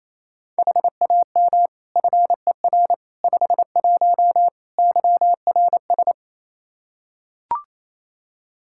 Morse code, heard as a keyed tone, about 27 wpm, 700 Hz. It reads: HAM FER 51 YRS
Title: Practice QSO - Exchange 3 - More Info - 28wpm